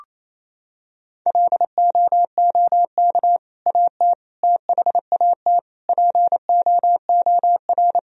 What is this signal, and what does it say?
Morse code, heard as a keyed tone, about 27 wpm, 700 Hz. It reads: LOOK AT THAT POOR
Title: Sentences from Top 1000 Words - Review Part 2 - 28wpm